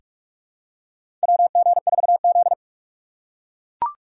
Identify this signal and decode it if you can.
Morse code, heard as a keyed tone, about 44 wpm, 690 Hz. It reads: WG4Z
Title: USA Call Signs - 4 Characters - 2x Character Spacing - 45wpm